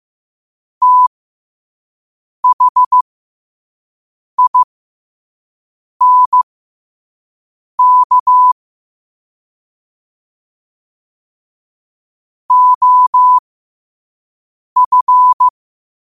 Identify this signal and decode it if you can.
Morse code, heard as a keyed tone, about 15 wpm, spaced as for 5 wpm, 1000 Hz. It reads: THINK OF
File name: 5wpm